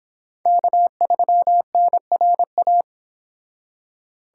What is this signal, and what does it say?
Morse code, heard as a keyed tone, about 26 wpm, 700 Hz. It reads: K3NRA